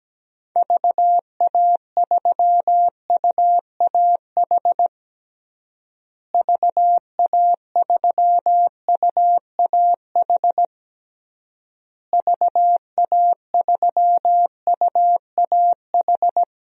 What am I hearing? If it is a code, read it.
Morse code, heard as a keyed tone, about 17 wpm, 700 Hz. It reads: VA3UAH VA3UAH VA3UAH
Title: DXCC-100 Call Signs - 6 Characters - Repeated 3x - 17wpm